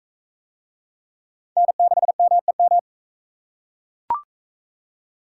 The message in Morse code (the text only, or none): N6MEM